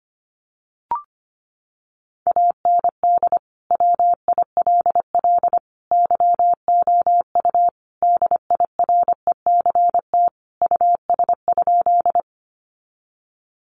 Morse code, heard as a keyed tone, about 25 wpm, 700 Hz. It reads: AND WILL YOU DIRECT US?
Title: Sentences from Top 400 Words 25wpm